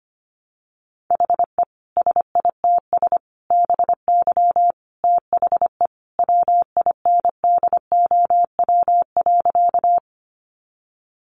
Morse code, heard as a keyed tone, about 25 wpm, 700 Hz. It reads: HE SITS BY THE WINDOW.
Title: Additional Easy Phrases 25wpm